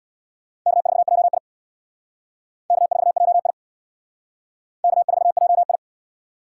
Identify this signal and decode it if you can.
Morse code, heard as a keyed tone, about 61 wpm, 700 Hz. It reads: K4JI K4JI K4JI
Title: Speed Racing - USA 1x2 Call Signs 50wpm